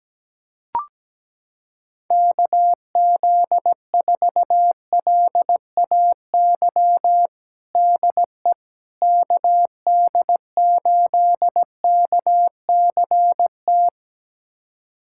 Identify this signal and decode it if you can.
Morse code, heard as a keyed tone, about 17 wpm, 700 Hz. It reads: KZ4LAY DE KD8KCT
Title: USA Callsign DE Callsign - 6 Characters - 17wpm